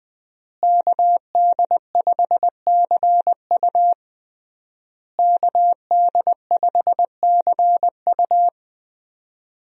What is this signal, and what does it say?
Morse code, heard as a keyed tone, about 21 wpm, 700 Hz. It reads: KD5CU KD5CU
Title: USA Call Signs - 5 Characters - Repeated 3x - 20wpm